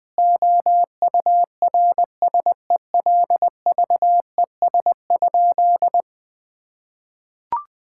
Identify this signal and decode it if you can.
Morse code, heard as a keyed tone, about 19 wpm, 700 Hz. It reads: OURSELVES?
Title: Sentences from Top 1600 Words 20wpm